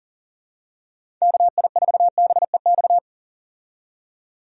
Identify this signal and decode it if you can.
Morse code, heard as a keyed tone, about 40 wpm, 700 Hz. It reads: KI4BEX
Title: USA Call Signs - 6 Characters - 40wpm